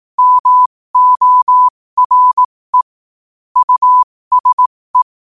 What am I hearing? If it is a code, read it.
Morse code, heard as a keyed tone, about 18 wpm, spaced as for 15 wpm, 1000 Hz. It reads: MORE USE
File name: qotdfiles / 15wpm